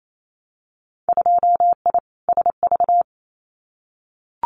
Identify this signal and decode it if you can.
Morse code, heard as a keyed tone, about 28 wpm, 700 Hz. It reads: 2I SV
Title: Winter Field Day Exchanges Encore 28wpm